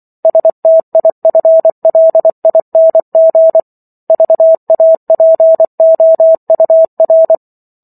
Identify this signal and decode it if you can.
Morse code, heard as a keyed tone, about 24 wpm, 650 Hz. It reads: STIFLING VAPOUR